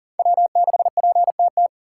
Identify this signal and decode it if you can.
Morse code, heard as a keyed tone, about 40 wpm, 700 Hz. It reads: W6PTT
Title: USA Call Signs - 5 Characters - 6x Character Spacing - 40wpm